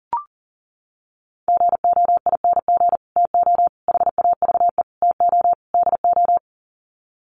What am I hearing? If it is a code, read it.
Morse code, heard as a keyed tone, about 40 wpm, 700 Hz. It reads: GOING TO HAVE TO DO
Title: Sets of 5 Words Encore 40wpm